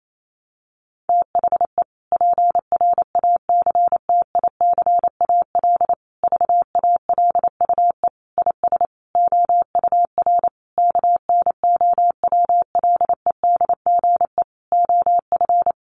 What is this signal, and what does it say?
Morse code, heard as a keyed tone, about 28 wpm, 700 Hz. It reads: THE PRACTICAL VALUE IS OUR KNOWLEDGE OF
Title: Sentences from Top 1400 Words 28wpm